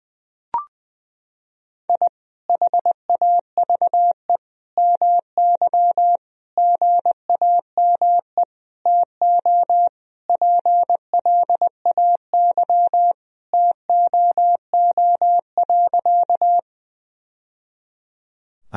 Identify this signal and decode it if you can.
Morse code, heard as a keyed tone, about 20 wpm, 700 Hz. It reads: I HAVE MY GAME TO PLAY TOO.
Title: Sentences from Top 1000 Words - Review Part 3 - 20wpm